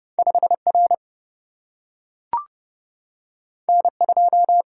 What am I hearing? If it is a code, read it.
Morse code, heard as a keyed tone, about 32 wpm, 700 Hz. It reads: 5 R N 2
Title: Single-Letter Single-Number 30wpm